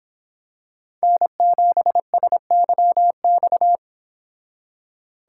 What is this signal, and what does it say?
Morse code, heard as a keyed tone, about 25 wpm, 700 Hz. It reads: N7SYX